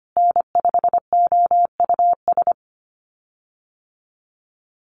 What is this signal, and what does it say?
Morse code, heard as a keyed tone, about 25 wpm, 700 Hz. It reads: N5OUS
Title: USA Call Signs Repeated 2x 25wpm